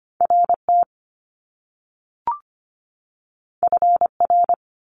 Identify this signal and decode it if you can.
Morse code, heard as a keyed tone, about 25 wpm, 700 Hz. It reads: RT FR
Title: Top Two Letter Combinations in Words 25wpm